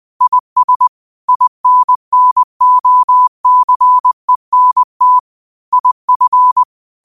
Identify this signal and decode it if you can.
Morse code, heard as a keyed tone, about 20 wpm, 1000 Hz. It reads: IS INNOCENT IF
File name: qotdfiles / 20wpm